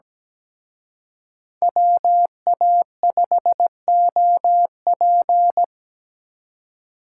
Morse code, heard as a keyed tone, about 20 wpm, 700 Hz. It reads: WA5OP
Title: Speed Racing - USA 2x2 Call Signs 17wpm